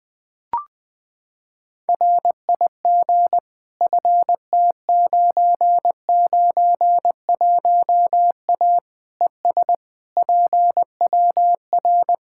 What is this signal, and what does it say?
Morse code, heard as a keyed tone, about 21 wpm, 700 Hz. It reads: RIG FT991A ES PWR
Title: Practice QSO - Exchange 2 - Info - 20wpm